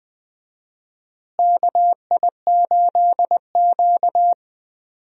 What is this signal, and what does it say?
Morse code, heard as a keyed tone, about 20 wpm, 700 Hz. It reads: KI8Q
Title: USA Call Signs - 4 Characters - Repeated 2x - 20wpm